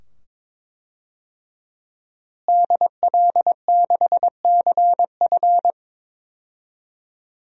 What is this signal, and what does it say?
Morse code, heard as a keyed tone, about 22 wpm, 700 Hz. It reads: DL6CF
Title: DXCC-100 Call Signs - 5 Characters - Repeated 2x - 22wpm